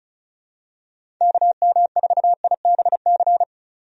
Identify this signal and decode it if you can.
Morse code, heard as a keyed tone, about 34 wpm, 700 Hz. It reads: KM4IBC
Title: USA Call Signs - 6 Characters - 35wpm